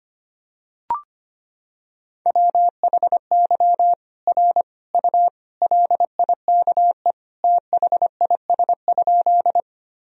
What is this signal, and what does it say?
Morse code, heard as a keyed tone, about 25 wpm, 700 Hz. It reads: WHY R U LIKE THIS?
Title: Sentences from Top 200 Words and Abbreviations 25wpm